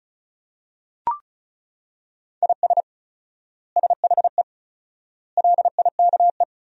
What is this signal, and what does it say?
Morse code, heard as a keyed tone, about 34 wpm, 700 Hz. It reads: IS SHE LIKE
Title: 4x Word-Spacing - Sentences from Top 100 Words - 35wpm